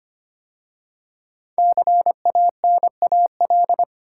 Morse code, heard as a keyed tone, about 25 wpm, 700 Hz. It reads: CANAL
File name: 5 Letter Words - First 3 Letters Given - 25wpm